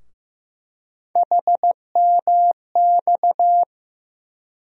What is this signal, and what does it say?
Morse code, heard as a keyed tone, about 15 wpm, 700 Hz. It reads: HMX
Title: USA Call Sign Postfixes - 3 Letters 15wpm